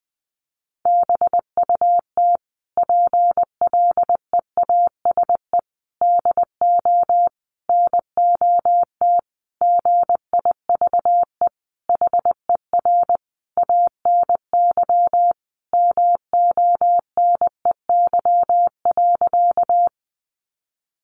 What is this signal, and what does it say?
Morse code, heard as a keyed tone, about 20 wpm, 700 Hz. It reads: BUT PLEASE DO NOT GIVE HER ANY MONEY.
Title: Sentences from Top 1500 Words 20wpm